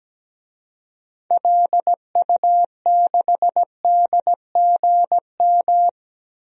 Morse code, heard as a keyed tone, about 17 wpm, 700 Hz. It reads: LU6DGM
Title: DXCC-100 Call Signs - 6 Characters - 17wpm